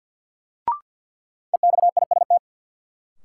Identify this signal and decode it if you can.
Morse code, heard as a keyed tone, about 52 wpm, 700 Hz. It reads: EXIST
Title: Sentences from Top 1000 Words - Review Part 5 - New Word Drill - 50wpm